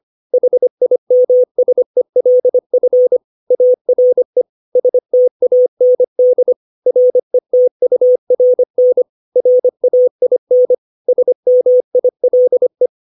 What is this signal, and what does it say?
Morse code, heard as a keyed tone, about 25 wpm, 500 Hz. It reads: HIMSELF ARE STAND RETURN RAIN SMILE
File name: 632 words / 632 Words 25WPM